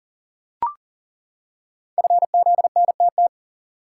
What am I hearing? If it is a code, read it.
Morse code, heard as a keyed tone, about 38 wpm, 690 Hz. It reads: FZNTT
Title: Code Groups - 5 Letters - 40wpm